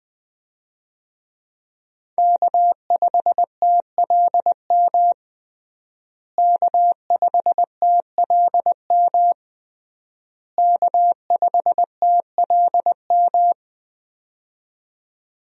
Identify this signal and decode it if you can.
Morse code, heard as a keyed tone, about 19 wpm, 700 Hz. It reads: K5TLM K5TLM K5TLM
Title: USA Call Signs - 5 Characters - Repeated 3x - 20wpm